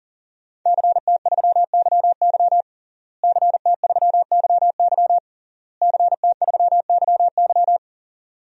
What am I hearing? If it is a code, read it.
Morse code, heard as a keyed tone, about 37 wpm, 700 Hz. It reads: CT3YY CT3YY CT3YY
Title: DXCC-100 Call Signs - 5 Characters - Repeated 3x - 40wpm